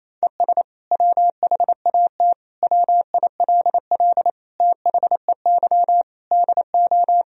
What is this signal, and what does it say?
Morse code, heard as a keyed tone, about 28 wpm, 700 Hz. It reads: ES WHAT WILL THEY DO
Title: Sentences from Top 200 Words and Abbreviations 28wpm